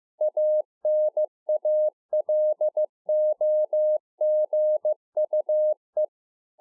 Morse code, heard as a keyed tone, about 15 wpm, 610 Hz. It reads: ANALOGUE